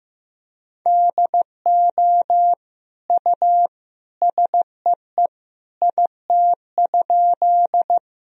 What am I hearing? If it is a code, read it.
Morse code, heard as a keyed tone, about 15 wpm, 700 Hz. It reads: DO U SEE IT?
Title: Sentences from Top 100 Words and Abbreviations 15wpm